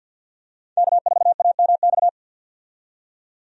Morse code, heard as a keyed tone, about 52 wpm, 700 Hz. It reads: K4AMX
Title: USA Call Signs - 5 Characters - 50wpm